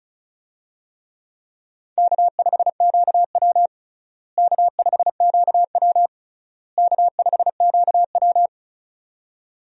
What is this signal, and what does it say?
Morse code, heard as a keyed tone, about 34 wpm, 700 Hz. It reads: K5QW K5QW K5QW
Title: USA Call Signs - 4 Characters - Repeated 3x - 35wpm